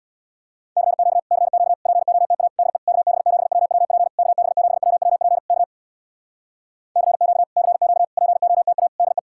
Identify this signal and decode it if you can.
Morse code, heard as a keyed tone, about 66 wpm, 700 Hz. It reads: CQ CQ POTA DE KB7WWP KB7WWP K CQ CQ POTA DE